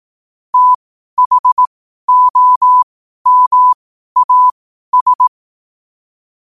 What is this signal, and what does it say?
Morse code, heard as a keyed tone, about 18 wpm, spaced as for 12 wpm, 1000 Hz. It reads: THOMAS
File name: qotdfiles / 12wpm